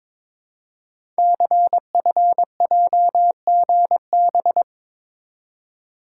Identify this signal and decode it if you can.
Morse code, heard as a keyed tone, about 22 wpm, 700 Hz. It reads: CFJGB
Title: Code Groups - 5 Letters - 22wpm